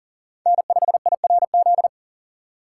Morse code, heard as a keyed tone, about 37 wpm, 700 Hz. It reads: N5IRZ
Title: USA Call Signs - 3x Character Spacing - 40wpm